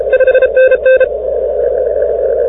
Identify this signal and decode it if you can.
Morse code, heard as a keyed tone, about 33 wpm, 1550 Hz, with noise. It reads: 5NN